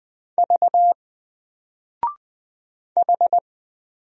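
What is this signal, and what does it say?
Morse code, heard as a keyed tone, about 20 wpm, 700 Hz. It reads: V H
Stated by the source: Single letter and number Morse drill at 20 wpm, from N4LCD.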